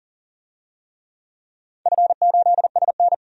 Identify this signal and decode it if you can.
Morse code, heard as a keyed tone, about 41 wpm, 690 Hz. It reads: F8SN
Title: DXCC-100 Call Signs - 4 Characters - 40wpm